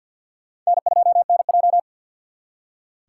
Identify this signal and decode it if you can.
Morse code, heard as a keyed tone, about 61 wpm, 700 Hz. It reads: N2NJ
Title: Speed Racing - USA 1x2 Call Signs 50wpm